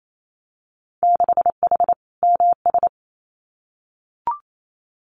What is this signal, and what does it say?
Morse code, heard as a keyed tone, about 28 wpm, 700 Hz. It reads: 6H MS
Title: Winter Field Day Exchanges Encore 28wpm